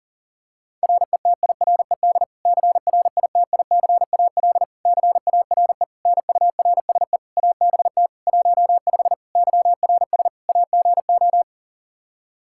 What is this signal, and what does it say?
Morse code, heard as a keyed tone, about 40 wpm, 700 Hz. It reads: RETIRED CRITICAL CARE NURSE ABT 15 YRS AGO
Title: Practice QSO - Exchange 3 - More Info - 40wpm